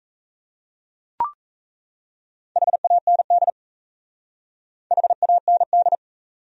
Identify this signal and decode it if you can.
Morse code, heard as a keyed tone, about 34 wpm, 700 Hz. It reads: HAND HAND
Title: Speed-Racing Top 500 Words 28wpm